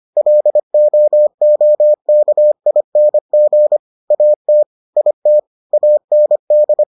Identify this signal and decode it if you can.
Morse code, heard as a keyed tone, about 25 wpm, 600 Hz. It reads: LOOKING AT IT AND